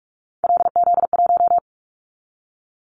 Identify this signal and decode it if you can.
Morse code, heard as a keyed tone, about 46 wpm, 700 Hz. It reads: LZ1
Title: Callsign Prefixes for Top 50 DXCC - 45wpm